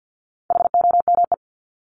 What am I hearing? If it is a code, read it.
Morse code, heard as a keyed tone, about 50 wpm, 700 Hz. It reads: HOME
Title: Top 300 Words 50wpm